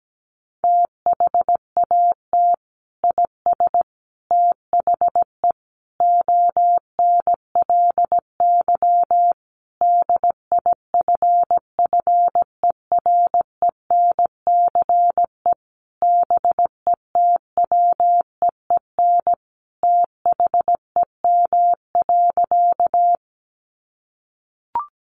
Morse code, heard as a keyed tone, about 17 wpm, 700 Hz. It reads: THAT IS THE ONLY DIFFERENCE BETWEEN THEM.
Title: Sentences from Top 800 Words 17wpm